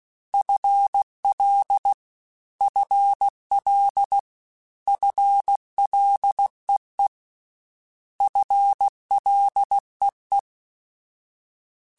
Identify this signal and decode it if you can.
Morse code, heard as a keyed tone, about 16 wpm, 800 Hz. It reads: FL FL FLEE FLEE
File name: FL / FL mp3s / FL Prefixes 3x2 Twice @15